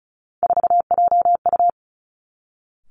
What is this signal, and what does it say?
Morse code, heard as a keyed tone, about 36 wpm, 700 Hz. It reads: 4JU
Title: Three Challenging Letter-Number Combinations 35wpm